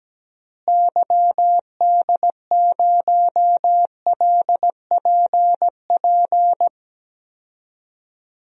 Morse code, heard as a keyed tone, about 17 wpm, 700 Hz. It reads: YD0LPP
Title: DXCC-100 Call Signs - 6 Characters - Repeated 2x - 17wpm